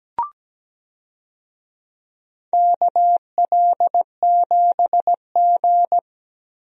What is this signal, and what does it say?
Morse code, heard as a keyed tone, about 17 wpm, 700 Hz. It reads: KL7G
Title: DXCC-100 Call Signs - 4 Characters - Repeated 3x - 17wpm